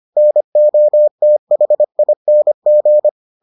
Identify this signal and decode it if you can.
Morse code, heard as a keyed tone, about 25 wpm, 600 Hz. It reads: NOTHING